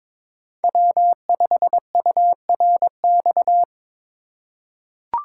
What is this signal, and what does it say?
Morse code, heard as a keyed tone, about 22 wpm, 700 Hz. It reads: W5URX
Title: USA Call Signs - 5 Characters - 3x Character Spacing - 22wpm